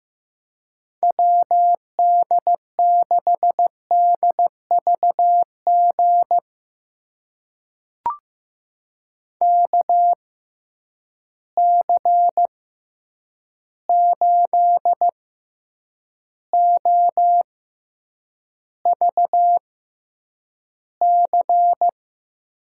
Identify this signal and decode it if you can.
Morse code, heard as a keyed tone, about 15 wpm, spaced as for 8 wpm, 700 Hz. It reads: WD6DVG KC8OVC
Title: USA Call Signs - 6 Characters - 6x Character Spacing - 15wpm